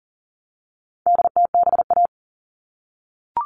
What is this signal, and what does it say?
Morse code, heard as a keyed tone, about 40 wpm, 700 Hz. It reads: DTBA
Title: All Cut Numbers - 4-Digit 40wpm